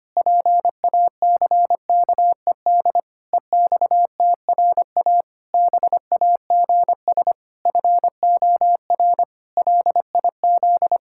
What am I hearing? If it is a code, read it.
Morse code, heard as a keyed tone, about 25 wpm, 700 Hz. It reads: PACKED EXTRA BAGS FOR LIZ
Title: Pangram Challenge v2 - 25wpm